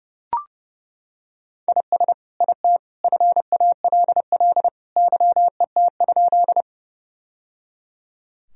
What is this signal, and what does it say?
Morse code, heard as a keyed tone, about 30 wpm, 700 Hz. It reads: IS IT FALL YET?
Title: Sentences from Top 1000 Words - Review Part 2 - 30wpm